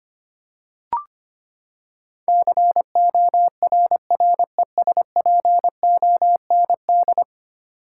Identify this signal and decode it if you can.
Morse code, heard as a keyed tone, about 25 wpm, 700 Hz. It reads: CORRESPOND
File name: Ten Letter Words - 25wpm